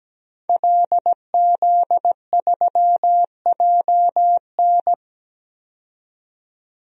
Morse code, heard as a keyed tone, about 17 wpm, 700 Hz. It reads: LZ3JN
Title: DXCC-100 Call Signs - 5 Characters - 17wpm